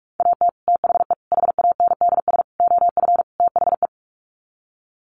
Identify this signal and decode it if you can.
Morse code, heard as a keyed tone, about 44 wpm, 690 Hz. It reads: AT THE HANDS OF THE
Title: Sets of 5 Words Encore2 45wpm